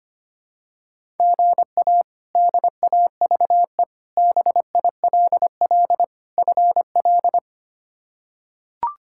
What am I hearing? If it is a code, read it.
Morse code, heard as a keyed tone, about 25 wpm, 700 Hz. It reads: GA DAVE BILL FL
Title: K1USN SST QSO Practice - 25wpm